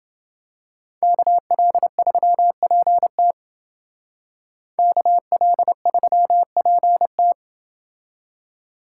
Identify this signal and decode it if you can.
Morse code, heard as a keyed tone, about 29 wpm, 700 Hz. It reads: KL3PT KL3PT